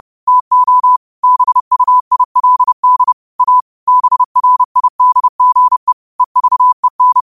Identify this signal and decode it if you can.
Morse code, heard as a keyed tone, about 30 wpm, 1000 Hz. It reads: TO BUILD A BRIDGE EVEN
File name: qotdfiles / 30wpm